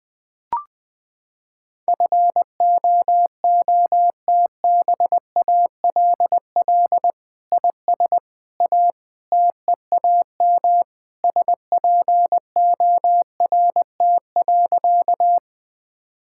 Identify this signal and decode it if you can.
Morse code, heard as a keyed tone, about 20 wpm, 700 Hz. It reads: FOOTBALL IS A TEAM SPORT.
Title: Sentences from Top 1600 Words 20wpm